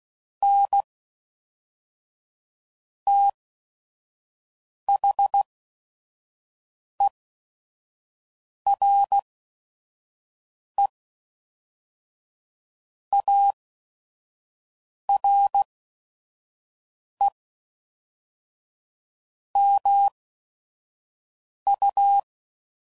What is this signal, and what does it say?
Morse code, heard as a keyed tone, about 16 wpm, spaced as for 5 wpm, 800 Hz. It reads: N THERE ARE MU
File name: CW Code Groups @5WPM / Technical Subjects / AND Gate